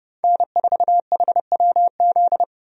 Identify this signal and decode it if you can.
Morse code, heard as a keyed tone, about 32 wpm, 700 Hz. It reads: N4HWZ
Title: USA Call Signs - 5 Characters - Repeated 2x - 30wpm